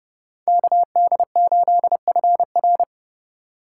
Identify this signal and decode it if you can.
Morse code, heard as a keyed tone, about 36 wpm, 700 Hz. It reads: KD8FR
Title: Speed Racing - USA 2x2 Call Signs 30wpm